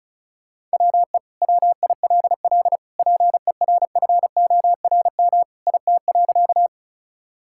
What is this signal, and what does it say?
Morse code, heard as a keyed tone, about 36 wpm, 700 Hz. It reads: WE WILL PERFORM IT.
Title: Sentences from Top 1000 Words - Review Part 5 - 35wpm